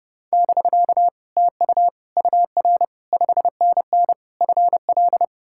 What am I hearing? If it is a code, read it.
Morse code, heard as a keyed tone, about 37 wpm, 700 Hz. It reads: <BK> TU UR 5NN FL
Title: Speed-Racing Common QSO Phrases - 30wpm